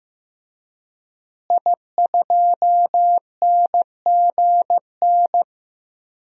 Morse code, heard as a keyed tone, about 15 wpm, 700 Hz. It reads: I2NGN
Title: DXCC-100 Call Signs - 5 Characters - Repeated 2x - 15wpm